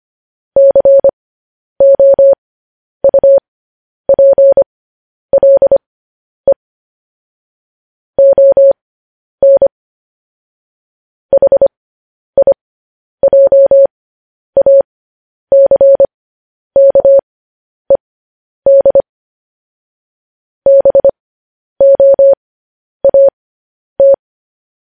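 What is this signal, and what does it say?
Morse code, heard as a keyed tone, about 25 wpm, spaced as for 10 wpm, 550 Hz. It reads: COUPLE ON HIJACKED BOAT
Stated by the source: Morse news headlines sent at 10 wpm.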